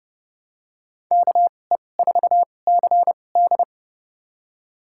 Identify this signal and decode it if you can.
Morse code, heard as a keyed tone, about 30 wpm, spaced as for 22 wpm, 700 Hz. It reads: KE4CD
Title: USA Call Signs - 2x Character Spacing - 30wpm